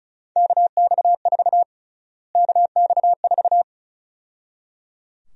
Morse code, heard as a keyed tone, about 34 wpm, 700 Hz. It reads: KX4 KX4
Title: USA Call Sign Prefixes - Repeated 2x - 35wpm